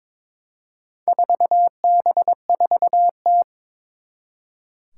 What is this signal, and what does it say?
Morse code, heard as a keyed tone, about 23 wpm, 700 Hz. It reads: 4B4T
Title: Four Challenging Letter-Number Combinations 22wpm